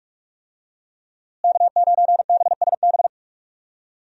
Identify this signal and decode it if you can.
Morse code, heard as a keyed tone, about 46 wpm, 700 Hz. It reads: K9BSB
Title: USA Call Signs - 5 Characters - Repeated 3x - 45wpm